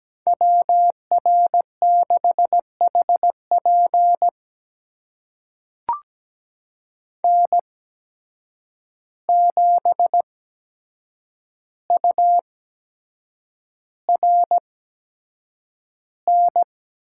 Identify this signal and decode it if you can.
Morse code, heard as a keyed tone, about 17 wpm, spaced as for 7 wpm, 700 Hz. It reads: WR6HP N7URN
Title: USA Call Signs - 5 Characters - 8x Character Spacing - 17wpm